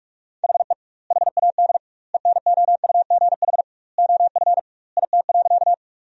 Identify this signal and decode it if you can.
Morse code, heard as a keyed tone, about 45 wpm, 700 Hz. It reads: HE HAD ENOUGH OF IT.
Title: 2x Word-Spacing - Sentences from Top 400 Words - 45wpm